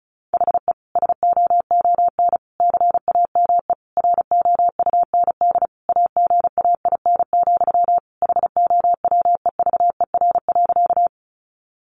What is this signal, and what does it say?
Morse code, heard as a keyed tone, about 34 wpm, 700 Hz. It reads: HE SOON CAME ROUND AGAIN, HOWEVER.
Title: Sentences from Top 600 Words 35wpm